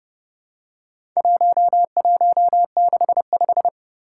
Morse code, heard as a keyed tone, about 29 wpm, 700 Hz. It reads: 1165
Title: Four Challenging Letter-Number Combinations 30wpm